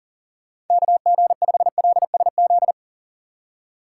Morse code, heard as a keyed tone, about 48 wpm, 700 Hz. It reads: KG5LSZ
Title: Speed Racing - USA 2x3 Call Signs 40wpm